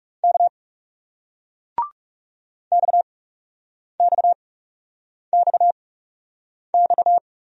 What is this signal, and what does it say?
Morse code, heard as a keyed tone, about 39 wpm, spaced as for 8 wpm, 700 Hz. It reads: K XXXX